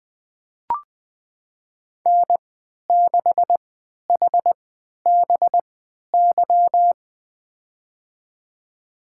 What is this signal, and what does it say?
Morse code, heard as a keyed tone, about 21 wpm, spaced as for 11 wpm, 700 Hz. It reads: N6HBY
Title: USA Call Signs - 3x Character Spacing - 20wpm